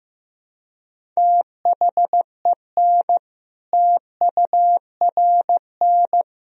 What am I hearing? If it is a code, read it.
Morse code, heard as a keyed tone, about 15 wpm, 700 Hz. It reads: THEN TURN
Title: Sentences from Top 1000 Words - Review Part 1 - 15wpm